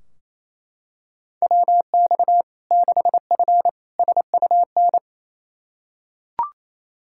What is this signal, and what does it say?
Morse code, heard as a keyed tone, about 28 wpm, 700 Hz. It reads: WX 6F SUN
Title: Practice QSO - Exchange 2 - Info - 28wpm